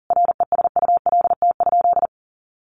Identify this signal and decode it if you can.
Morse code, heard as a keyed tone, about 40 wpm, 700 Hz. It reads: RESULT?
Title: Sentences from Top 1500 Words 40wpm